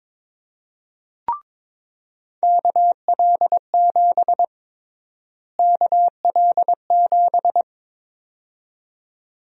Morse code, heard as a keyed tone, about 22 wpm, 700 Hz. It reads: KL7 KL7
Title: USA Call Sign Prefixes - Repeated 2x - 22wpm